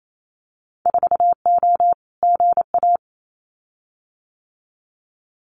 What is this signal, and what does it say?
Morse code, heard as a keyed tone, about 28 wpm, 700 Hz. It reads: 4O GA
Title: Winter Field Day Exchanges 28wpm